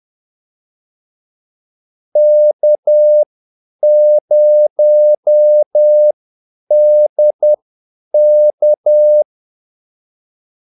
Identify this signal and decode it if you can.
Morse code, heard as a keyed tone, about 10 wpm, spaced as for 8 wpm, 600 Hz. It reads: K0DK
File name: Callsigns-10-8wpm